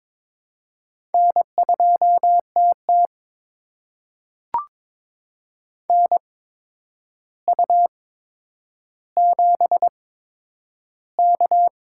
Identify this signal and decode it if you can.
Morse code, heard as a keyed tone, about 22 wpm, spaced as for 9 wpm, 700 Hz. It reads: N2TT NU7K
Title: USA Call Signs - 8x Character Spacing - 22wpm